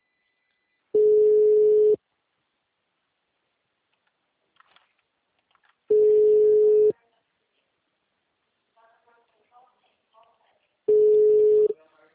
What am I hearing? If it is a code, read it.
Ringback tone (Europe, 425 Hz), with speech.
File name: out- -2009- - - .6